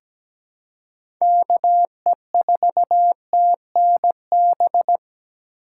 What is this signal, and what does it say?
Morse code, heard as a keyed tone, about 17 wpm, 700 Hz. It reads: KE4TNB
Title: USA Call Signs - 4x Character Spacing - 17wpm